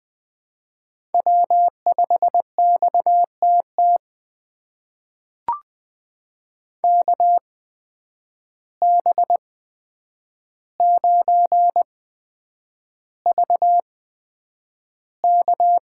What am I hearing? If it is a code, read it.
Morse code, heard as a keyed tone, about 20 wpm, spaced as for 9 wpm, 700 Hz. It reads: W5XTT KB9VK
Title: USA Call Signs - 5 Characters - 8x Character Spacing - 20wpm